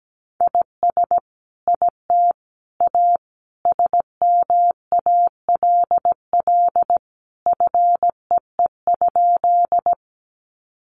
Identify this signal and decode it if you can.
Morse code, heard as a keyed tone, about 17 wpm, 700 Hz. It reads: IS IT A SMALL FEE?
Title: Sentences from Top 1500 Words 17wpm